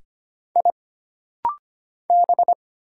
Morse code, heard as a keyed tone, about 25 wpm, 700 Hz. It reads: I B
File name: Single Character (Letter:Number:Punctuation:Prosign) - Rapid-Fire - 25wpm